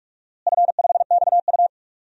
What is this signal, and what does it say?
Morse code, heard as a keyed tone, about 46 wpm, 700 Hz. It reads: F5XU
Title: DXCC-100 Call Signs - 4 Characters - Repeated 2x - 45wpm